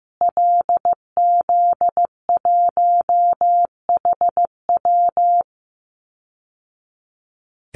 Morse code, heard as a keyed tone, about 15 wpm, 700 Hz. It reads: LZ1HW
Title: Call Signs Repeated 2x - 15wpm